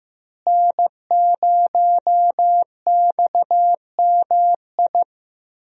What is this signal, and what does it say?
Morse code, heard as a keyed tone, about 15 wpm, 700 Hz. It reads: N0XMI
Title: USA Call Signs - 5 Characters - Repeated 2x - 15wpm